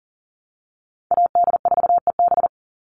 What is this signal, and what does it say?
Morse code, heard as a keyed tone, about 40 wpm, 700 Hz. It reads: AD4EB
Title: Call Signs - 40wpm